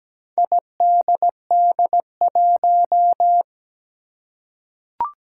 Morse code, heard as a keyed tone, about 17 wpm, 700 Hz. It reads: IDD1
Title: Four Challenging Letter-Number Combinations 17wpm